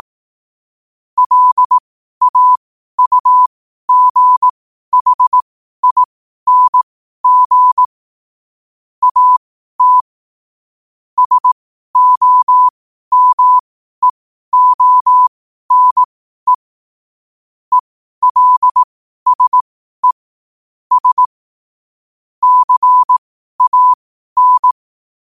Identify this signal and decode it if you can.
Morse code, heard as a keyed tone, about 18 wpm, spaced as for 12 wpm, 1000 Hz. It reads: LAUGHING AT SOMEONE ELSE S CAN
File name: qotdfiles / 12wpm